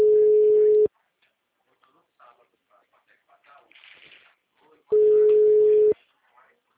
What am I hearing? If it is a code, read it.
Ringback tone (Europe, 425 Hz), with speech.